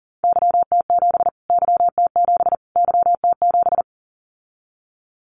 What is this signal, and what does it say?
Morse code, heard as a keyed tone, about 40 wpm, 700 Hz. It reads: YT7 YT7 YT7
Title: Callsign Prefixes for Top 25 DXCC - Repeated 3x - 40wpm